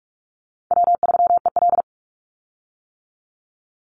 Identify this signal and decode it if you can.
Morse code, heard as a keyed tone, about 44 wpm, 700 Hz. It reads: W3EL
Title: Call Signs - 45wpm